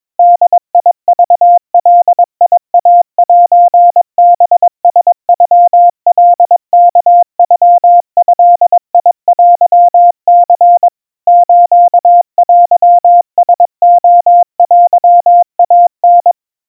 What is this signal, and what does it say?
Morse code (Japanese Wabun code), heard as a keyed tone, about 22 wpm, 700 Hz. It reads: ボクガイセハラノカワノドテニ ステラレテイタ
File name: 80 cpm 700 Hz Crispy